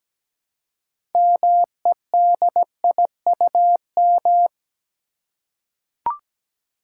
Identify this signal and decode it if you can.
Morse code, heard as a keyed tone, about 17 wpm, 700 Hz. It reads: MEDIUM
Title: Six Letter Words - 17wpm